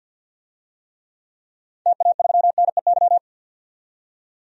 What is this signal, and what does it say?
Morse code, heard as a keyed tone, about 50 wpm, 700 Hz. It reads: TA3NEY
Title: DXCC-100 Call Signs - 6 Characters - 50wpm